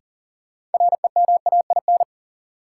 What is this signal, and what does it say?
Morse code, heard as a keyed tone, about 40 wpm, 700 Hz. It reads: REMAIN
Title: Sentences from Top 1200 Words - New Word Drill - 40wpm